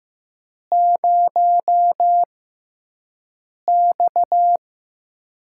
Morse code, heard as a keyed tone, about 15 wpm, spaced as for 5 wpm, 700 Hz. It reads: 0X
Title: USA Call Signs - 4 Characters - 6x Character Spacing - 15wpm